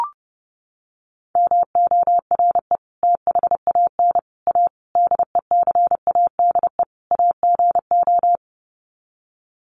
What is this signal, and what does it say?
Morse code, heard as a keyed tone, about 30 wpm, 700 Hz. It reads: MORE THAN A DECADE AGO
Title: Sets of 5 Words Encore2 30wpm